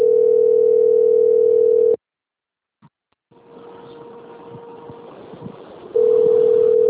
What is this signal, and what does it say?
Ringback tone (North America), 440 + 480 Hz.